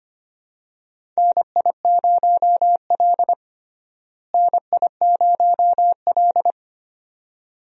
Morse code, heard as a keyed tone, about 25 wpm, 700 Hz. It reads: NI0L NI0L
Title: USA Call Signs - 4 Characters - Repeated 2x - 25wpm